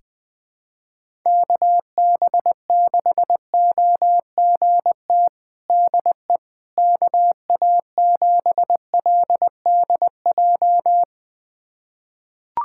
Morse code, heard as a keyed tone, about 20 wpm, 700 Hz. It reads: KB6OGT DE KA7LDJ
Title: USA Callsign DE Callsign - 6 Characters - 20wpm